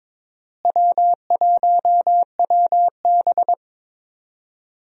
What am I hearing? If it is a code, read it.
Morse code, heard as a keyed tone, about 22 wpm, 700 Hz. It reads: W1WB
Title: USA Call Signs - 4 Characters - Repeated 3x - 22wpm